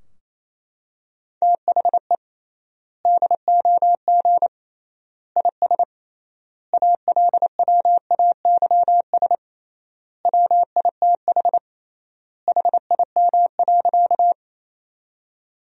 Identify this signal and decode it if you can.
Morse code, heard as a keyed tone, about 28 wpm, 700 Hz. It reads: THE DOG IS ALWAYS WITH HIM.
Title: 3x Word-Spacing - Sentences from Top 400 Words - 28wpm